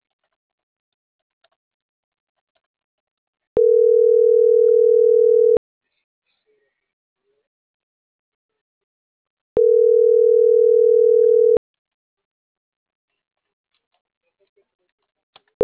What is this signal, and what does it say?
Ringback tone (North America), 440 + 480 Hz.